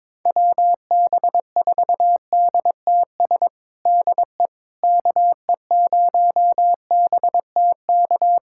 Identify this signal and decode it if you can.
Morse code, heard as a keyed tone, about 21 wpm, 700 Hz. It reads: WB4DTS DE KE0BTK
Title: USA Callsign DE Callsign - 6 Characters - 22wpm